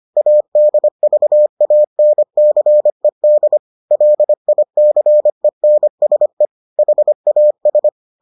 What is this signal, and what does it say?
Morse code, heard as a keyed tone, about 25 wpm, 600 Hz. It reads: ADVANCED LICENSE HAS